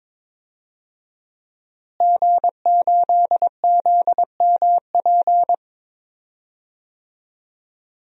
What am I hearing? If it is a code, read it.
Morse code, heard as a keyed tone, about 22 wpm, 700 Hz. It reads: G8ZMP
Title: DXCC-100 Call Signs - 5 Characters - Repeated 3x - 22wpm